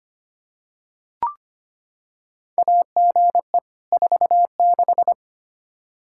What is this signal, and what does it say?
Morse code, heard as a keyed tone, about 25 wpm, 700 Hz. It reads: AGE 46
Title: Practice QSO - Exchange 3 - More Info - 25wpm